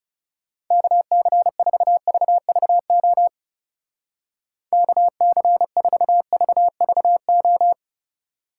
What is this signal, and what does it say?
Morse code, heard as a keyed tone, about 38 wpm, 700 Hz. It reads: KC4VVO KC4VVO